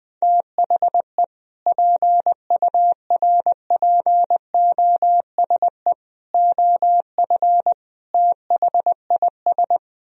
Morse code, heard as a keyed tone, about 20 wpm, 700 Hz. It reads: THE PURPOSE OF THIS
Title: Speed-Racing - Sets of 5 Words 20wpm